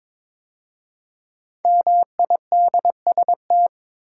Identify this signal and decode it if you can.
Morse code, heard as a keyed tone, about 23 wpm, 700 Hz. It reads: MIDST
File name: 5 Letter Words - First 3 Letters Given - 22wpm